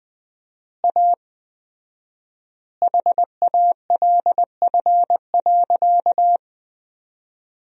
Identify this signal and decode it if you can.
Morse code, heard as a keyed tone, about 20 wpm, 700 Hz. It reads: A HALF.
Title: 4x Word-Spacing - Sentences from Top 400 Words - 20wpm